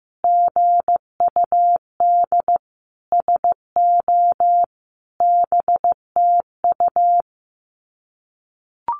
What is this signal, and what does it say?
Morse code, heard as a keyed tone, about 14 wpm, 700 Hz. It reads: GUD SO BTU
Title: Common QSO Phrases 15wpm